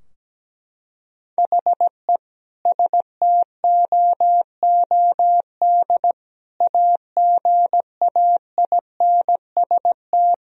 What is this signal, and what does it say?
Morse code, heard as a keyed tone, about 17 wpm, 700 Hz. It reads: HE STOOD AGAINST
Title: Sentences from Top 1000 Words - Review Part 2 - 17wpm